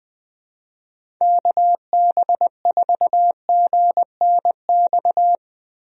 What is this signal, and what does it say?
Morse code, heard as a keyed tone, about 21 wpm, 700 Hz. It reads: KB4GNX
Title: USA Call Signs - 6 Characters - Repeated 2x - 20wpm